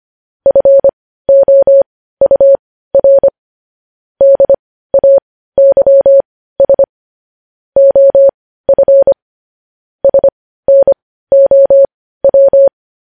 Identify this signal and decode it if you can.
Morse code, heard as a keyed tone, about 25 wpm, spaced as for 15 wpm, 550 Hz. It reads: FOUR DAYS OF SNOW
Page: News Headlines in Morse Code at 15 WPM